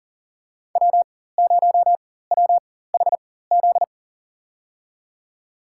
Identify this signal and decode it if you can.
Morse code, heard as a keyed tone, about 40 wpm, spaced as for 19 wpm, 700 Hz. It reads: W0WHZ
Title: USA Call Signs - 5 Characters - 4x Character Spacing - 40wpm